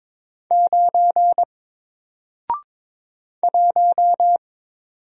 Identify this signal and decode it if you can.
Morse code, heard as a keyed tone, about 22 wpm, 700 Hz. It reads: 9 1
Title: Single Character (Letter:Number:Punctuation:Prosign) Alt 22wpm